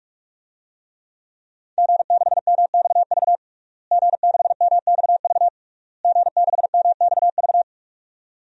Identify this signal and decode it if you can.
Morse code, heard as a keyed tone, about 41 wpm, 700 Hz. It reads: G6MXV G6MXV G6MXV
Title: DXCC-100 Call Signs - 5 Characters - Repeated 3x - 45wpm